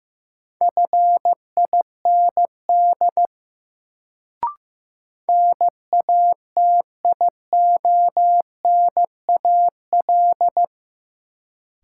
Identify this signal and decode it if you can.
Morse code, heard as a keyed tone, about 15 wpm, 700 Hz. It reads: FIND NATIONAL
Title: Sets of 2 Words - New Word Drill - 15wpm